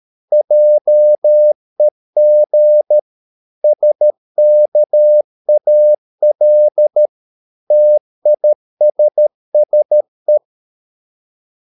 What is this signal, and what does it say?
Morse code, heard as a keyed tone, about 13 wpm, 600 Hz. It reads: JEG SKAL TISSE
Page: Ferie 13 wpm